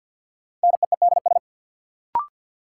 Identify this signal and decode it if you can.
Morse code, heard as a keyed tone, about 53 wpm, 700 Hz. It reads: NEEDS
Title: Sentences from Top 1600 Words - New Word Drill Alt - 50wpm